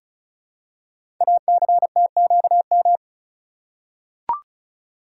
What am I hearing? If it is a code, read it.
Morse code, heard as a keyed tone, about 34 wpm, 700 Hz. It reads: ACTQM
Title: Code Groups - 5 Letters - 35wpm